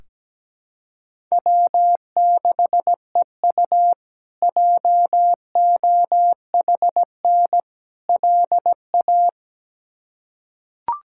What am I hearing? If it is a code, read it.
Morse code, heard as a keyed tone, about 17 wpm, 700 Hz. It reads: W6EU JOHN LA
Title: K1USN SST QSO Practice - 17wpm